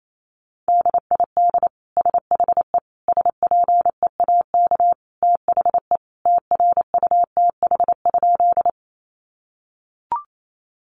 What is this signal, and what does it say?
Morse code, heard as a keyed tone, about 28 wpm, 700 Hz. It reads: DID SHE SPEAK THE TRUTH?
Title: Sentences from Top 1100 Words 28wpm